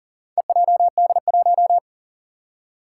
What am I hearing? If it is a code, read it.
Morse code, heard as a keyed tone, about 40 wpm, 700 Hz. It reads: EJD1
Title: Four Challenging Letter-Number Combinations 40wpm